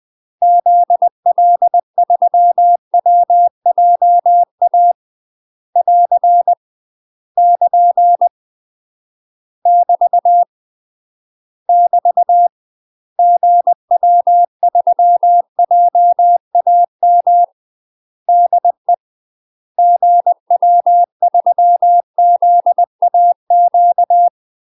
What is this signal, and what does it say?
Morse code, heard as a keyed tone, about 20 wpm, 700 Hz. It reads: ZL3WJA <AR> <KN> <BT> <BT> GW3JAM DE GW3ZAQ
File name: 20WPM